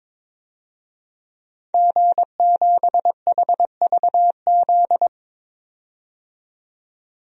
Morse code, heard as a keyed tone, about 22 wpm, 700 Hz. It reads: G7HVZ
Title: DXCC-100 Call Signs - 5 Characters - 22wpm